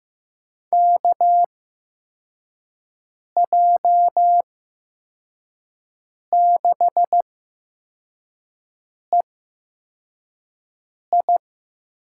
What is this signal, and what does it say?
Morse code, heard as a keyed tone, about 15 wpm, spaced as for 4 wpm, 700 Hz. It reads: KJ6EI
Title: USA Call Signs - 6 Characters - 8x Character Spacing - 15wpm